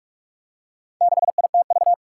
Morse code, heard as a keyed tone, about 46 wpm, 700 Hz. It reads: 6ITV
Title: Four Challenging Letter-Number Combinations 45wpm